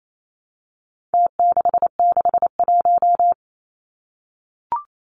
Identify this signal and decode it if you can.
Morse code, heard as a keyed tone, about 28 wpm, 700 Hz. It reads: T661
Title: Numbers - Contest Style - 4-Digit 28wpm